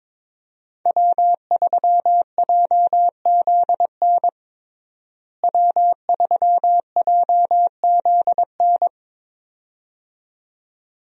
Morse code, heard as a keyed tone, about 23 wpm, 700 Hz. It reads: W3JZN W3JZN
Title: USA Call Signs - 5 Characters - Repeated 2x - 22wpm